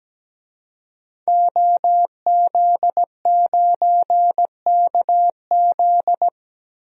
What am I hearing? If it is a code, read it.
Morse code, heard as a keyed tone, about 17 wpm, 700 Hz. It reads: OZ9KZ
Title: DXCC-100 Call Signs - 5 Characters - 17wpm